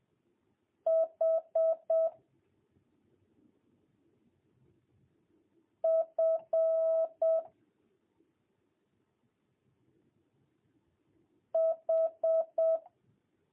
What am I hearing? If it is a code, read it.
Morse code, heard as a keyed tone, about 7 wpm, 650 Hz. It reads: H F H
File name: jh8cbh / 2022cw